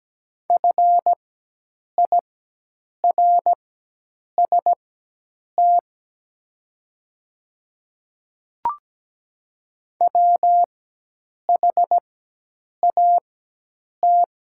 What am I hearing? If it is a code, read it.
Morse code, heard as a keyed tone, about 17 wpm, spaced as for 8 wpm, 700 Hz. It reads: FIRST WHAT
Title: 4x Character Spacing - Top 100 Words - 17wpm